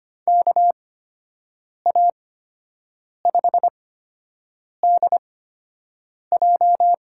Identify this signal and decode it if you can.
Morse code, heard as a keyed tone, about 25 wpm, spaced as for 7 wpm, 700 Hz. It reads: KA5DJ
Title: USA Call Signs - 6 Characters - 8x Character Spacing - 25wpm